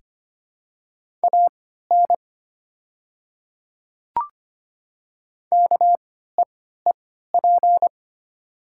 Morse code, heard as a keyed tone, about 25 wpm, spaced as for 14 wpm, 690 Hz. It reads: AN KEEP
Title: 3x Character Spacing - Top 300 Words - 25wpm